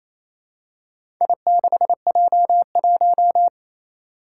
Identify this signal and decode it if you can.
Morse code, heard as a keyed tone, about 28 wpm, 700 Hz. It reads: I6J1
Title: Four Challenging Letter-Number Combinations 28wpm